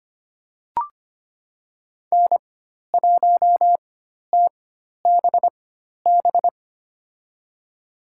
Morse code, heard as a keyed tone, about 25 wpm, spaced as for 12 wpm, 700 Hz. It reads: N1TBB
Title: USA Call Signs - 5 Characters - 4x Character Spacing - 25wpm